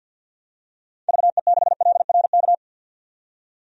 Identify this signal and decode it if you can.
Morse code, heard as a keyed tone, about 53 wpm, 700 Hz. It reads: VE6LRK
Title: DXCC-100 Call Signs - 6 Characters - 50wpm